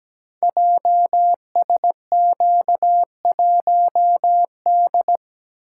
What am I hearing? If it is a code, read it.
Morse code, heard as a keyed tone, about 17 wpm, 700 Hz. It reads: JSQ1D
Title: Code Groups - 5 Letter:Numbers - 17wpm